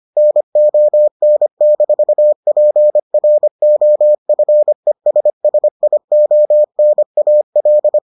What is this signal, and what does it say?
Morse code, heard as a keyed tone, about 25 wpm, 600 Hz. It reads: NON-PROFESSIONAL